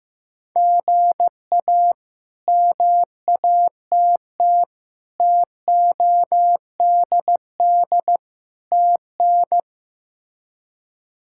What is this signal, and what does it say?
Morse code, heard as a keyed tone, about 15 wpm, 700 Hz. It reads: GA MATT TODD TN
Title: K1USN SST QSO Practice - 15wpm